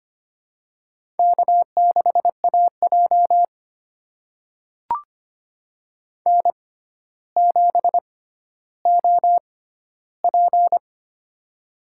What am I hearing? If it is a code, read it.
Morse code, heard as a keyed tone, about 25 wpm, spaced as for 13 wpm, 700 Hz. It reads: K6AJ N7OP
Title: USA Call Signs - 4 Characters - 6x Character Spacing - 25wpm